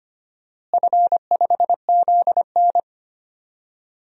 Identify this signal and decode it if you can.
Morse code, heard as a keyed tone, about 25 wpm, 700 Hz. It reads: F5ZN
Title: DXCC-100 Call Signs - 4 Characters - Repeated 3x - 25wpm